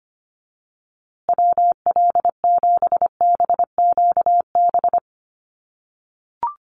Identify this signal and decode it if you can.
Morse code, heard as a keyed tone, about 25 wpm, 700 Hz. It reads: WL7BQB
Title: USA Call Signs Repeated 3x 25wpm